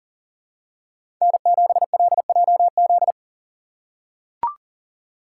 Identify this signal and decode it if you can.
Morse code, heard as a keyed tone, about 40 wpm, 700 Hz. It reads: N7LJZ
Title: USA Call Signs - 5 Characters - 6x Character Spacing - 40wpm